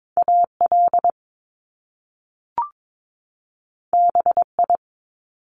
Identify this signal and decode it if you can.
Morse code, heard as a keyed tone, about 22 wpm, 700 Hz. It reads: AL BI
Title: Top Two Letter Combinations in Words 22wpm